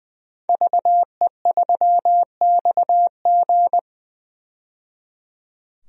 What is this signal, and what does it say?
Morse code, heard as a keyed tone, about 21 wpm, 700 Hz. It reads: VE3XG
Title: DXCC-100 Call Signs - 5 Characters - 20wpm